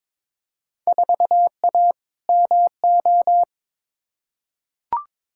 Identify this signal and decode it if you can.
Morse code, heard as a keyed tone, about 22 wpm, 700 Hz. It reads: 4A MO
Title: Field Day Exchanges - 22wpm